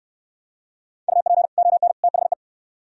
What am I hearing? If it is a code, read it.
Morse code, heard as a keyed tone, about 66 wpm, 700 Hz. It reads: UP ON THE